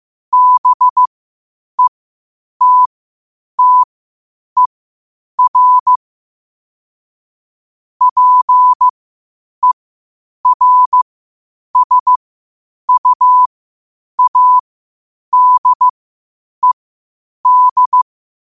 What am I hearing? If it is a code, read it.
Morse code, heard as a keyed tone, about 15 wpm, spaced as for 8 wpm, 1000 Hz. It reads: BETTER PERSUADED
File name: qotdfiles / 8wpm